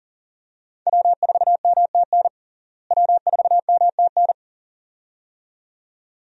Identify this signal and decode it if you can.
Morse code, heard as a keyed tone, about 40 wpm, 700 Hz. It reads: W4MTN W4MTN
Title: USA Call Signs - 5 Characters - Repeated 2x - 40wpm